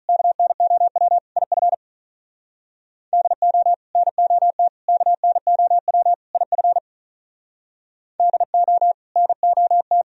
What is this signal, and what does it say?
Morse code, heard as a keyed tone, about 45 wpm, 700 Hz. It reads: KNOW IF DO NOT KNOW IF DO NOT